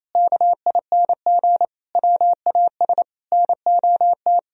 Morse code, heard as a keyed tone, about 27 wpm, 700 Hz. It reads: KING WAS NOT
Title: Sentences from Top 1000 Words - Review Part 3 - 28wpm